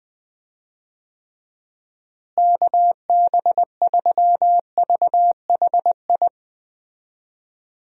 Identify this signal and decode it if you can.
Morse code, heard as a keyed tone, about 20 wpm, 700 Hz. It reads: KB3VHI
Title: USA Call Signs - 6 Characters - Repeated 2x - 20wpm